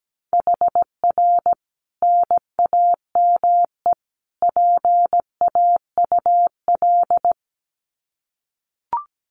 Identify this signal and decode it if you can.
Morse code, heard as a keyed tone, about 17 wpm, 700 Hz. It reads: HR NAME PAUL
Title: Common QSO Phrases 17wpm